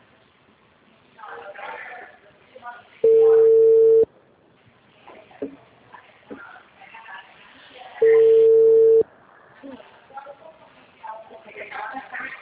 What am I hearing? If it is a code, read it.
Ringback tone (China, 450 Hz), with speech.
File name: out- -2025- - - .17516